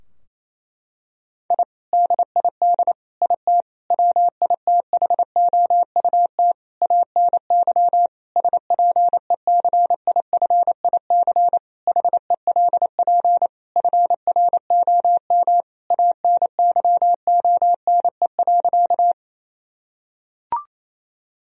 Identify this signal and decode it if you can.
Morse code, heard as a keyed tone, about 28 wpm, 700 Hz. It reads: I DID IT WITHOUT ANY SPECIFIC HELP FROM ANYONE.
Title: Sentences from Top 1000 Words - Review Part 3 - 28wpm